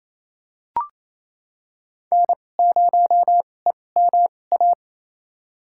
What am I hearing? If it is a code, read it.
Morse code, heard as a keyed tone, about 28 wpm, spaced as for 20 wpm, 700 Hz. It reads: N0EMA
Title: USA Call Signs - 5 Characters - 2x Character Spacing - 28wpm